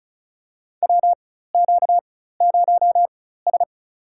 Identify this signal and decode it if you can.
Morse code, heard as a keyed tone, about 35 wpm, spaced as for 16 wpm, 700 Hz. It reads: WQ0S
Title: USA Call Signs - 4 Characters - 4x Character Spacing - 35wpm